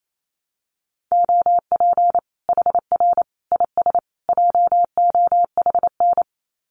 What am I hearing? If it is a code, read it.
Morse code, heard as a keyed tone, about 28 wpm, 700 Hz. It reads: OP HR IS JOHN
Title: Common QSO Phrases 28wpm